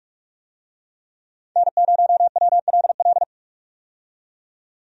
Morse code, heard as a keyed tone, about 42 wpm, 700 Hz. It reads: N0WLL
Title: USA Call Signs - 5 Characters - 45wpm